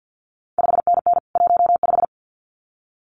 Morse code, heard as a keyed tone, about 50 wpm, 700 Hz. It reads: 5NN 15
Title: CQ WW DX Contest Exchange - 50wpm